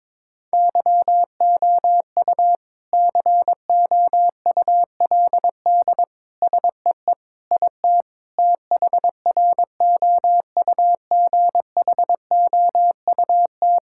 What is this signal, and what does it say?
Morse code, heard as a keyed tone, about 22 wpm, 700 Hz. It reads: YOU COULD SEE IT THROUGHOUT
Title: Sentences from Top 1000 Words - Review Part 5 - 22wpm